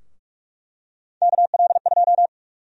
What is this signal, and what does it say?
Morse code, heard as a keyed tone, about 45 wpm, 700 Hz. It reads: KL2
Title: USA Call Sign Prefixes - 45wpm